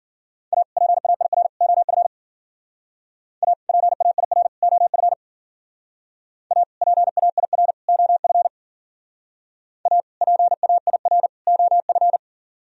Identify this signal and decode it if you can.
Morse code, heard as a keyed tone, about 49 wpm, 700 Hz. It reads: A PAIR OF A PAIR OF A PAIR OF A PAIR OF
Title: Speed-Racing - Sets of 3 Words 40wpm